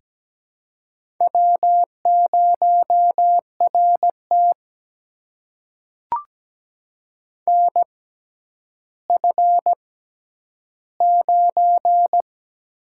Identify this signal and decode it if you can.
Morse code, heard as a keyed tone, about 17 wpm, spaced as for 10 wpm, 700 Hz. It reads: W0RT NF9
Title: USA Call Signs - 4 Characters - 6x Character Spacing - 17wpm